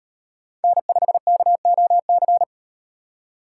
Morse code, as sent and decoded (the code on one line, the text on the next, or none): -. ..... -.- --- -.-.
N5KOC